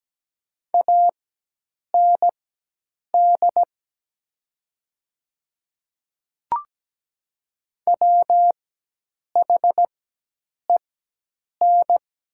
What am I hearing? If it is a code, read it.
Morse code, heard as a keyed tone, about 17 wpm, spaced as for 8 wpm, 700 Hz. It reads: AND WHEN
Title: 4x Character Spacing - Top 300 Words - 17wpm